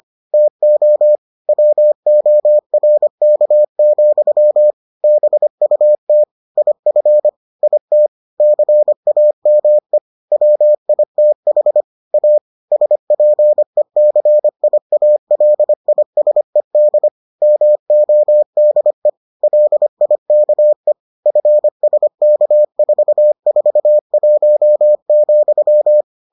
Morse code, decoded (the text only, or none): TO WORK, BUT IF IT CAME WITH A SPECIALISED MODE LIKE FSK441,